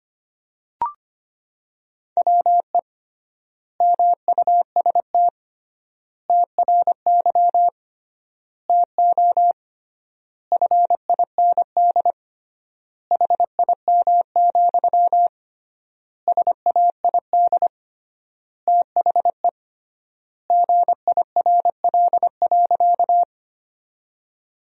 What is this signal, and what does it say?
Morse code, heard as a keyed tone, about 25 wpm, 700 Hz. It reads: WE MUST TRY TO FIND HIM, SAID THE GIRL.
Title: 3x Word-Spacing - Sentences from Top 400 Words - 25wpm